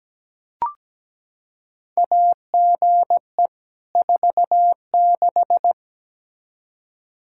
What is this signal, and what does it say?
Morse code, heard as a keyed tone, about 17 wpm, 700 Hz. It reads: AGE 46
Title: Practice QSO - Exchange 3 - More Info - 17wpm